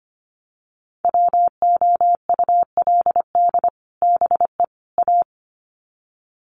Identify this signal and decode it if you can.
Morse code, heard as a keyed tone, about 25 wpm, 700 Hz. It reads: WOULD BE A
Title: Sets of 3 Words 25wpm